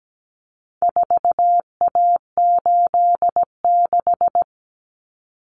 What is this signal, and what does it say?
Morse code, heard as a keyed tone, about 17 wpm, 700 Hz. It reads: 4A86
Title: Common Cut Numbers - 4-Digit 17wpm